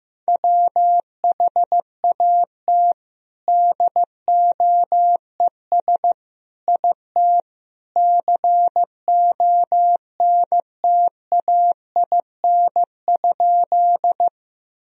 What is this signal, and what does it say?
Morse code, heard as a keyed tone, about 15 wpm, 700 Hz. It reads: WHAT DOES IT CONTAIN?
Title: Sentences from Top 1000 Words - Review Part 2 - 15wpm